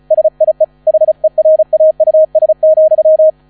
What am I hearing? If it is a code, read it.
Morse code, heard as a keyed tone, about 34 wpm, 630 Hz. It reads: SIE HERAUS,
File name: 35wpm